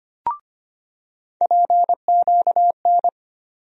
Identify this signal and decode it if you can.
Morse code, heard as a keyed tone, about 25 wpm, 700 Hz. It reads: PQN
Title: USA Call Sign Postfixes - 3 Letters 25wpm